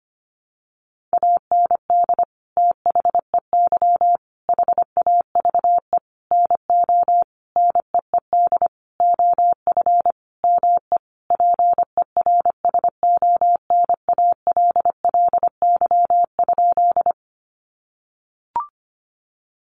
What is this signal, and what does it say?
Morse code, heard as a keyed tone, about 25 wpm, 700 Hz. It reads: AND THEY HAVE NO NEED OF ME PERSONALLY?
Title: Sentences from Top 1400 Words 25wpm